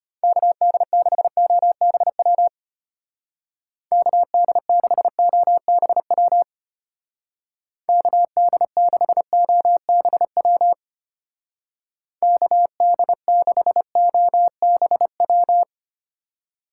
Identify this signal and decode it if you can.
Morse code, heard as a keyed tone, about 32 wpm, 700 Hz. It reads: KD6OBW KD6OBW KD6OBW KD6OBW